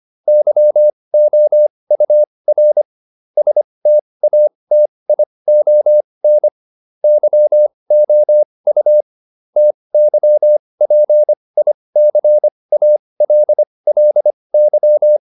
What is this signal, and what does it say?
Morse code, heard as a keyed tone, about 25 wpm, spaced as for 20 wpm, 600 Hz. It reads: YOUR STATION YOU TYPICALLY